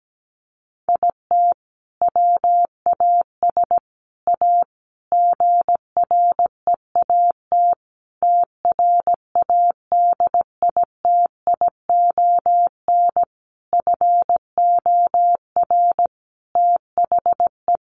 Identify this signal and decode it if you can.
Morse code, heard as a keyed tone, about 17 wpm, 700 Hz. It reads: IT WAS A GREAT TRADITION FOR THE
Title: Sentences from Top 1500 Words 17wpm